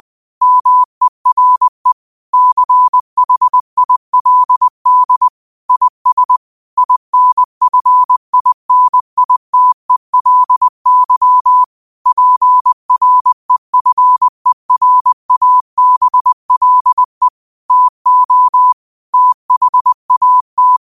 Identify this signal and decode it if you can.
Morse code, heard as a keyed tone, about 20 wpm, 1000 Hz. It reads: MERE CHILD IS INFINITELY PREFERABLE TO THAT
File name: qotdfiles / 20wpm